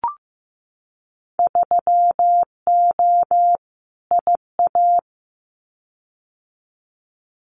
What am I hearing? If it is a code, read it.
Morse code, heard as a keyed tone, about 15 wpm, 700 Hz. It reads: 3O IA
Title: Winter Field Day Exchanges 15wpm